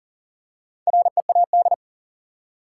Morse code, heard as a keyed tone, about 40 wpm, 700 Hz. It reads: READ
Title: Four Letter Words - 40wpm